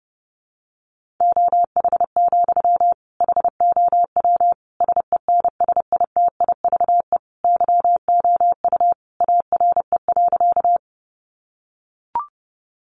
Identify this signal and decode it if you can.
Morse code, heard as a keyed tone, about 30 wpm, 700 Hz. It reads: OH, HOW SENSITIVE YOU ARE.
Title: Sentences from Top 1400 Words 30wpm